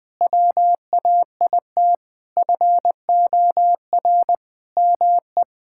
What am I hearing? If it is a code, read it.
Morse code, heard as a keyed tone, about 19 wpm, 700 Hz. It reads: WAIT FOR ME
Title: Sentences from Top 1000 Words - Review Part 6 - 20wpm